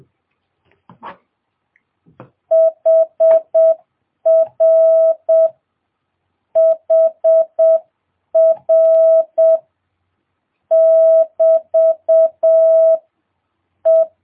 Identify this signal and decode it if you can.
Morse code, heard as a keyed tone, about 7 wpm, 650 Hz. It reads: HR HR <BT> E
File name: jh8cbh / 2022cw